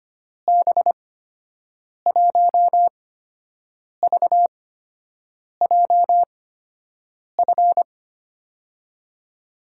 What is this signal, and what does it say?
Morse code, heard as a keyed tone, about 25 wpm, spaced as for 7 wpm, 700 Hz. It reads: B1VJF
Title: USA Call Signs - 6 Characters - 8x Character Spacing - 25wpm